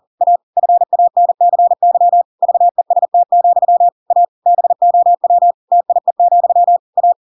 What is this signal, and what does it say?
Morse code, heard as a keyed tone, about 40 wpm, 700 Hz. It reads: A FANCY VEST, A BOW TIE, A
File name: Pinocchio / Pinocchio ch8 40 40 ik1hge